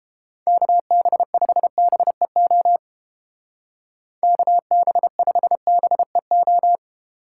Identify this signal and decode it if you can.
Morse code, heard as a keyed tone, about 30 wpm, 700 Hz. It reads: KB5BEO KB5BEO